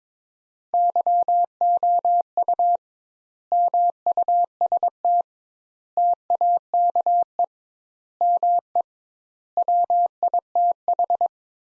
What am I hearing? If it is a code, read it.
Morse code, heard as a keyed tone, about 22 wpm, 700 Hz. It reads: YOU MUST TAKE ME WITH
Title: Sentences Top 200 Words - 22wpm